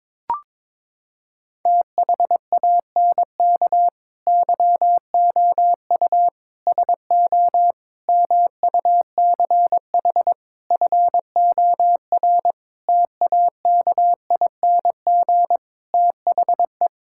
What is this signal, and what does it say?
Morse code, heard as a keyed tone, about 22 wpm, 700 Hz. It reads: THANK YOU SO MUCH FOR TAKING THE
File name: Sentences with 9-Word Phrases - 22wpm